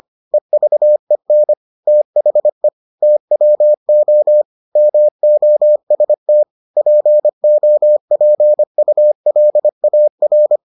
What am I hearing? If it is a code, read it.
Morse code, heard as a keyed tone, about 25 wpm, 600 Hz. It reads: EVEN THE TWO MOST POPULAR